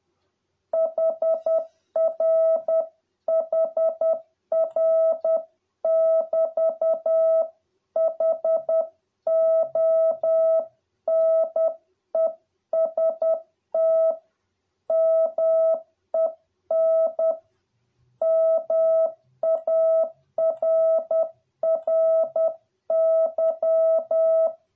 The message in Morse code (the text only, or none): HR HR <BT> HONEST MEN MARRY